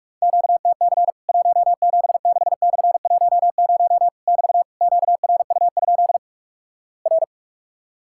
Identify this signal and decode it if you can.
Morse code, heard as a keyed tone, about 48 wpm, 700 Hz. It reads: QTC 176/10 <BT> QRU? R
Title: Worked All Europe DX Contest - QTC Exchange - 45wpm